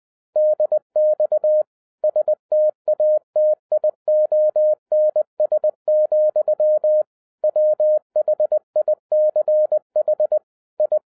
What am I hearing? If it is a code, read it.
Morse code, heard as a keyed tone, about 20 wpm, 610 Hz. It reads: DX STATIONS, WHICH I
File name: 2012-01-mp3 / n05-CAME BACK TO HAM RADIO AFTER 30 YEARS OF INTERVAL-20wpm